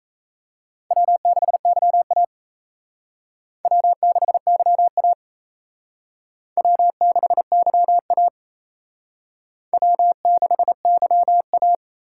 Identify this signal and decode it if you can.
Morse code, heard as a keyed tone, about 34 wpm, 700 Hz. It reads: W6YA W6YA W6YA W6YA
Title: Speed Racing - USA Call Signs 28wpm